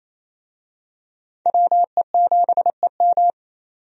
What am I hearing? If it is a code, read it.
Morse code, heard as a keyed tone, about 28 wpm, 700 Hz. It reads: WE7EM
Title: USA Call Signs - 5 Characters - 28wpm